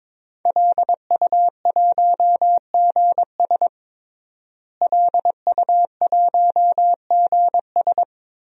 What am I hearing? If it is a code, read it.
Morse code, heard as a keyed tone, about 22 wpm, 700 Hz. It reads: LU1GS LU1GS
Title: DXCC-100 Call Signs - 5 Characters - Repeated 3x - 22wpm